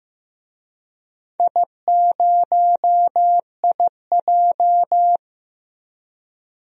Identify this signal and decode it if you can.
Morse code, heard as a keyed tone, about 15 wpm, 700 Hz. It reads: I0IJ
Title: DXCC-100 Call Signs - 4 Characters - Repeated 2x - 15wpm